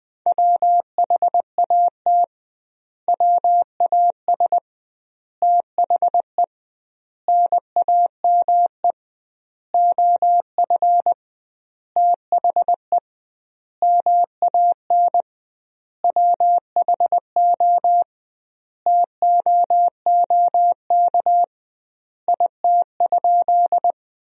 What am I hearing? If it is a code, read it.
Morse code, heard as a keyed tone, about 20 wpm, 700 Hz. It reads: WHAT WAS THE NAME OF THE MAN WHO TOOK IT?
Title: 2x Word-Spacing - Sentences from Top 300 Words - 20wpm